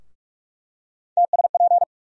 Morse code, heard as a keyed tone, about 46 wpm, 700 Hz. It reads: TSP
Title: USA Call Sign Postfixes - 3 Letters 45wpm